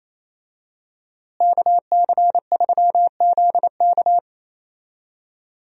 Morse code, heard as a keyed tone, about 28 wpm, 700 Hz. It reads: KC3ZK
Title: USA Call Signs - 5 Characters - Repeated 2x - 28wpm